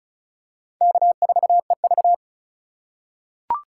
Morse code, heard as a keyed tone, about 36 wpm, 700 Hz. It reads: K4EV
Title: USA Call Signs - 3x Character Spacing - 35wpm